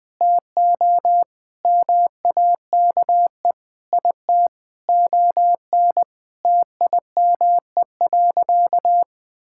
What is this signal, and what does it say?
Morse code, heard as a keyed tone, about 21 wpm, 700 Hz. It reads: TO MAKE IT ON TIME.
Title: Sentences with 3-Word Phrases - 20wpm